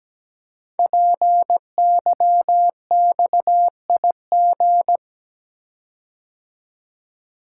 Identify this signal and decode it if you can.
Morse code, heard as a keyed tone, about 17 wpm, 700 Hz. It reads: PYXIG
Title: Code Groups - 5 Letters - 17wpm